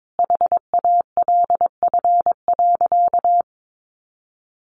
Morse code, heard as a keyed tone, about 22 wpm, 700 Hz. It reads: HALF.
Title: Sentences from Top 1100 Words 22wpm